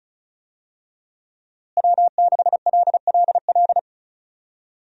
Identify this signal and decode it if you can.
Morse code, heard as a keyed tone, about 36 wpm, 700 Hz. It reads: W6LLL
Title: USA Call Signs - 5 Characters - Repeated 3x - 35wpm